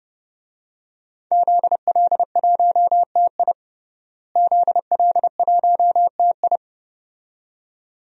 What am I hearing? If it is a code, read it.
Morse code, heard as a keyed tone, about 31 wpm, 700 Hz. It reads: ZL1TI ZL1TI
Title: DXCC-100 Call Signs - 5 Characters - Repeated 2x - 30wpm